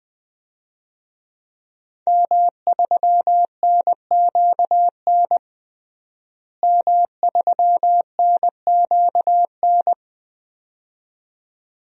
Morse code, heard as a keyed tone, about 21 wpm, 700 Hz. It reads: M3NQN M3NQN
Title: DXCC-100 Call Signs - 5 Characters - Repeated 2x - 20wpm